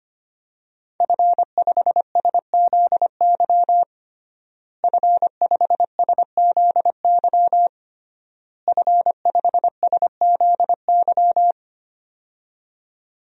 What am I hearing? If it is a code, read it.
Morse code, heard as a keyed tone, about 25 wpm, 700 Hz. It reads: F5SZY F5SZY F5SZY
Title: DXCC-100 Call Signs - 5 Characters - Repeated 3x - 25wpm